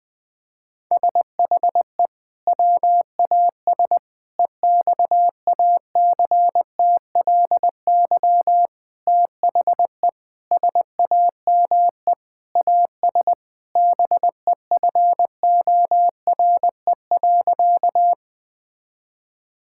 Morse code, heard as a keyed tone, about 20 wpm, 700 Hz. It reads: SHE WAS EXACTLY THE SAME AS BEFORE.
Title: Sentences from Top 1000 Words - Review Part 6 - 20wpm